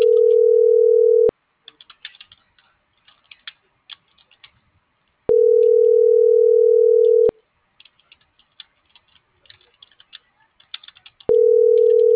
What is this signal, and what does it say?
Ringback tone (North America), 440 + 480 Hz.